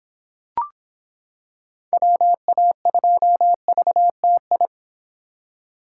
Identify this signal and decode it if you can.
Morse code, heard as a keyed tone, about 26 wpm, 700 Hz. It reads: WA2VTI